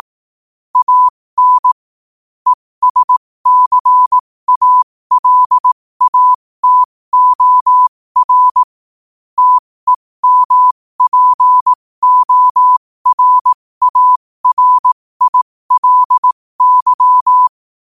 Morse code, heard as a keyed tone, about 18 wpm, spaced as for 15 wpm, 1000 Hz. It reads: AN ESCALATOR TEMPORARILY
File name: qotdfiles / 15wpm